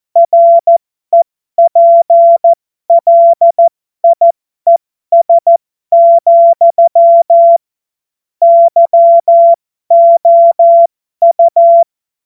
Morse code, heard as a keyed tone, about 14 wpm, spaced as for 12 wpm, 680 Hz. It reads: REPLIES, YOU
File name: Morsekurs / Witze